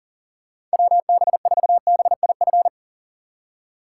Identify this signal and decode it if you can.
Morse code, heard as a keyed tone, about 40 wpm, 700 Hz. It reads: WB4BIF
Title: USA Call Signs - 6 Characters - 2x Character Spacing - 40wpm